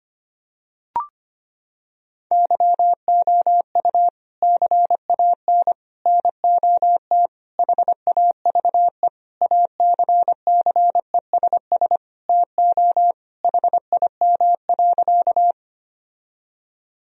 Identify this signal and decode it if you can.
Morse code, heard as a keyed tone, about 25 wpm, 700 Hz. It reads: YOU CAN NOT HAVE ACCESS TO HIM.
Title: Sentences from Top 1000 Words - Review Part 4 - 25wpm